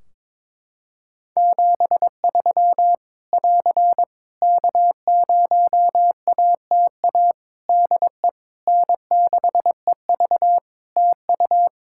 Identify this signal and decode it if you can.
Morse code, heard as a keyed tone, about 22 wpm, 700 Hz. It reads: 73 <AR> K0ATA DE N6EV TU
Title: Practice QSO - Exchange 4 - Ending - 22wpm